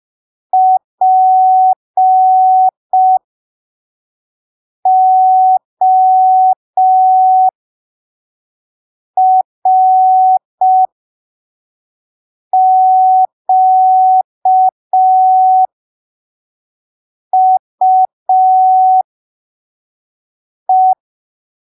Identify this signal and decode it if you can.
Morse code, heard as a keyed tone, about 5 wpm, spaced as for 3 wpm, 750 Hz. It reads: PORQUE